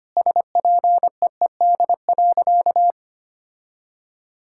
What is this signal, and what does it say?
Morse code, heard as a keyed tone, about 25 wpm, 700 Hz. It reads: SPEED.
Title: Sentences from Top 1000 Words - Review Part 5 - 25wpm